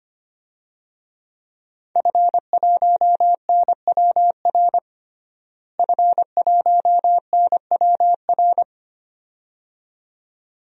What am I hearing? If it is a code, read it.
Morse code, heard as a keyed tone, about 25 wpm, 700 Hz. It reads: F1NWR F1NWR
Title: DXCC-100 Call Signs - 5 Characters - Repeated 2x - 25wpm